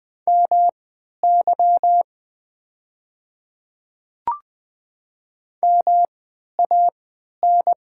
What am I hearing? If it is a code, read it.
Morse code, heard as a keyed tone, about 20 wpm, spaced as for 11 wpm, 700 Hz. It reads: MY MAN
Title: 3x Character Spacing - Top 500 Words - 20wpm